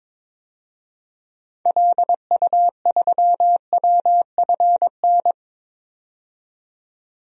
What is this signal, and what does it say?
Morse code, heard as a keyed tone, about 21 wpm, 700 Hz. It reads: LU3WFN
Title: DXCC-100 Call Signs - 6 Characters - 22wpm